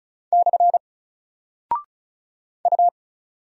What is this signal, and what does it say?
Morse code, heard as a keyed tone, about 36 wpm, 700 Hz. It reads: / U
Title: Single Character (Letter:Number:Punctuation:Prosign) Alt 35wpm